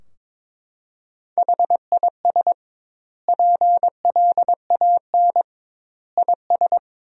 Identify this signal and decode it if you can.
Morse code, heard as a keyed tone, about 22 wpm, 700 Hz. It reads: HIS PLAN IS
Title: 2x Word-Spacing - Sentences from Top 500 Words - 22wpm